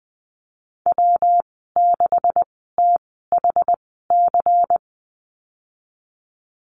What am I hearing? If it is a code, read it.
Morse code, heard as a keyed tone, about 20 wpm, spaced as for 15 wpm, 700 Hz. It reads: W6THC
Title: 128 Lesson 21g - Number 6 - Call Signs 20wpm